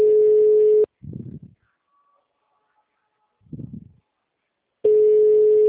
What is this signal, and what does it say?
Ringback tone (Europe, 425 Hz), with speech.
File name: out- -2011- - - .4